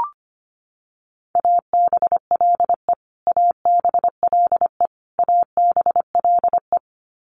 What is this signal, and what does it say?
Morse code, heard as a keyed tone, about 25 wpm, 700 Hz. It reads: ABLE ABLE ABLE
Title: Top 500 Words Repeated 25wpm